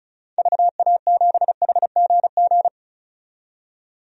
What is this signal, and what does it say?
Morse code, heard as a keyed tone, about 36 wpm, 700 Hz. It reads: VA7HGG
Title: DXCC-100 Call Signs - 6 Characters - 35wpm